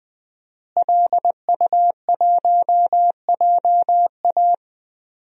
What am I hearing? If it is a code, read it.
Morse code, heard as a keyed tone, about 20 wpm, 700 Hz. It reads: LU1JA
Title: DXCC-100 Call Signs - 5 Characters - Repeated 3x - 20wpm